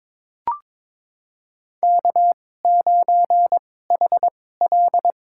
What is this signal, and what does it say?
Morse code, heard as a keyed tone, about 22 wpm, spaced as for 16 wpm, 700 Hz. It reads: K9HL
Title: USA Call Signs - 2x Character Spacing - 22wpm